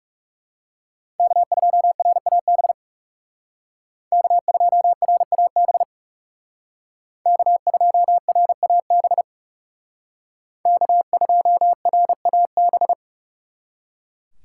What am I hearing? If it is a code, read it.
Morse code, heard as a keyed tone, about 37 wpm, 700 Hz. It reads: K2RAB K2RAB K2RAB K2RAB
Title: Speed Racing - USA 1x3 Call Signs 30wpm